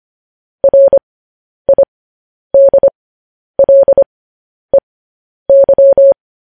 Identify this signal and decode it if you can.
Morse code, heard as a keyed tone, about 25 wpm, spaced as for 10 wpm, 550 Hz. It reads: RIDLEY
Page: news-headlines-in-morse-code-at-10-wpm